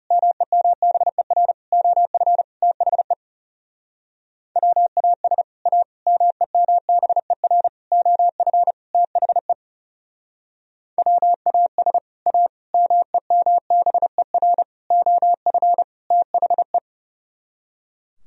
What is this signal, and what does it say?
Morse code, heard as a keyed tone, about 37 wpm, 700 Hz. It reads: MEMBER OF THE WAS A MEMBER OF THE WAS A MEMBER OF THE
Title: Speed-Racing - Sets of 5 Words 30wpm